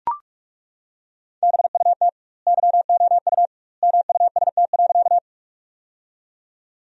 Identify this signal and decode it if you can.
Morse code, heard as a keyed tone, about 41 wpm, 700 Hz. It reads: BUT YOU MUST.
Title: 2x Word-Spacing - Sentences from Top 200 Words - 45wpm